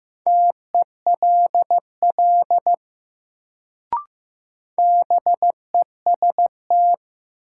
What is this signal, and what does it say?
Morse code, heard as a keyed tone, about 15 wpm, 700 Hz. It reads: TELL BEST
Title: Sets of 2 Words - New Word Drill - 15wpm